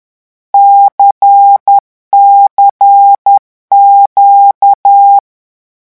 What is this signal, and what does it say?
Morse code, heard as a keyed tone, about 10 wpm, 800 Hz. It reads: CCQ
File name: morse mp3 / lecon2-vitesse 10